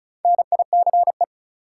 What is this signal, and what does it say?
Morse code, heard as a keyed tone, about 36 wpm, 700 Hz. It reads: NICE
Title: Sentences from Top 1100 Words - New Word Drill - 35wpm